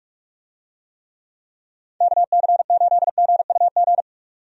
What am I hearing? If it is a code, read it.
Morse code, heard as a keyed tone, about 41 wpm, 700 Hz. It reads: KC8GUG
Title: USA Call Signs - 6 Characters - 45wpm